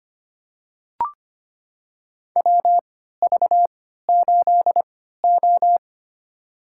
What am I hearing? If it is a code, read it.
Morse code, heard as a keyed tone, about 25 wpm, spaced as for 14 wpm, 700 Hz. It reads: WV8O
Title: USA Call Signs - 3x Character Spacing - 25wpm